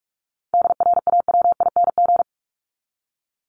Morse code, heard as a keyed tone, about 44 wpm, 700 Hz. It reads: DRAWING
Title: ING Suffix - Top 100 Words with Base Words 45wpm